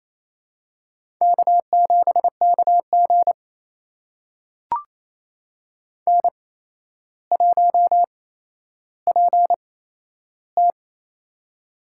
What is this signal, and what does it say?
Morse code, heard as a keyed tone, about 27 wpm, spaced as for 12 wpm, 700 Hz. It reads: K7KG N1PT
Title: USA Call Signs - 4 Characters - 8x Character Spacing - 28wpm